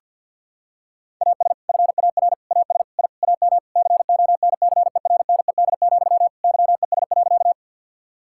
Morse code, heard as a keyed tone, about 47 wpm, 700 Hz. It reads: AS FAR AS I AM CONCERNED, YES.
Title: Sentences from Top 1000 Words - Review Part 6 - 50wpm